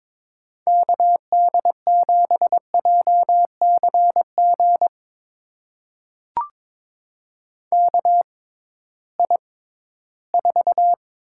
Morse code, heard as a keyed tone, about 22 wpm, spaced as for 14 wpm, 700 Hz. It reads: KD7JCG KI4
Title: USA Call Signs - 6x Character Spacing - 22wpm